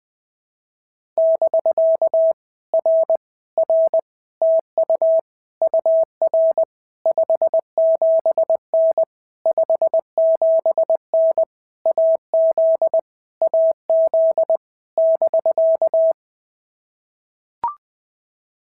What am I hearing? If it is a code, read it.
Morse code, heard as a keyed tone, about 20 wpm, 650 Hz. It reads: <BK> R R TU UR 57N 57N AZ AZ <BK>
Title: POTA QSO Practice - 20wpm